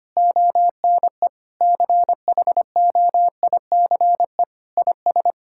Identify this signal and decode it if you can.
Morse code, heard as a keyed tone, about 25 wpm, 700 Hz. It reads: ONE CHOICE IS
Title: Sentences from Top 1000 Words - Review Part 6 - 25wpm